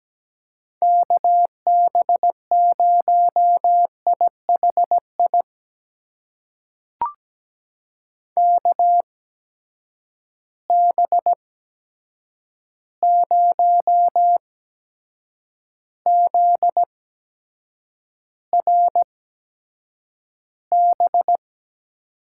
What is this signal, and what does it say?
Morse code, heard as a keyed tone, about 17 wpm, spaced as for 7 wpm, 700 Hz. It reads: KB0IHI KB0ZRB
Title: USA Call Signs - 6 Characters - 8x Character Spacing - 17wpm